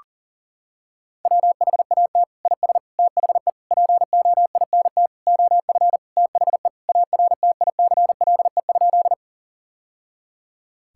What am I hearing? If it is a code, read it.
Morse code, heard as a keyed tone, about 40 wpm, 700 Hz. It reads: WHAT IS THE POINT OF THE ARTICLE?
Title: Sentences from Top 1000 Words - Review Part 5 - 40wpm